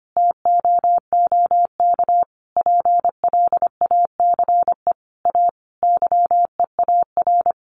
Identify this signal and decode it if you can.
Morse code, heard as a keyed tone, about 25 wpm, 700 Hz. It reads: TOOK PLACE A YEAR
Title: Sentences from Top 1500 Words 25wpm